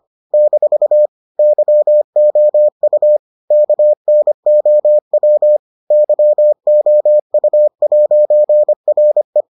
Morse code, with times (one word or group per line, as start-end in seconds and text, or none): 0.3-1.1 -
1.4-3.2 YOU
3.5-5.6 KNOW
5.9-9.4 YOU'RE